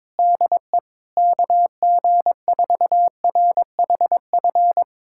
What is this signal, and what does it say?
Morse code, heard as a keyed tone, about 21 wpm, 700 Hz. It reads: DE KG4RHF
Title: USA Callsign DE Callsign - 6 Characters - 22wpm